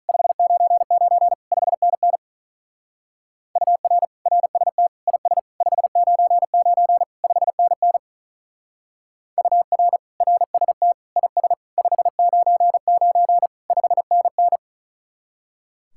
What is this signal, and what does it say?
Morse code, heard as a keyed tone, about 43 wpm, 700 Hz. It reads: 599 5NN UR RST IS 599 5NN UR RST IS 599 5NN
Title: Speed-Racing Common QSO Phrases Encore - 35wpm